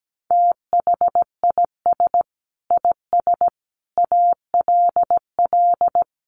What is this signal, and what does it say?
Morse code, heard as a keyed tone, about 17 wpm, 700 Hz. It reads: THIS IS ALL
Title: Sentences from Top 1400 Words 17wpm